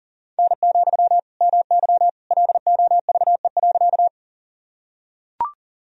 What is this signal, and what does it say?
Morse code, heard as a keyed tone, about 40 wpm, 700 Hz. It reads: N, MY LOVE.
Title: Sentences from Top 400 Words and Abbreviations 40wpm